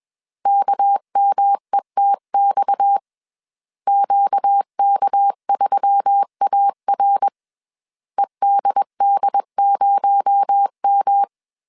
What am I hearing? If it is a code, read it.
Morse code, heard as a keyed tone, about 21 wpm, 800 Hz. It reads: KMETX QK3AR EDD0M
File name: CW / DET / 1 6 1500C S C 20wpm 2116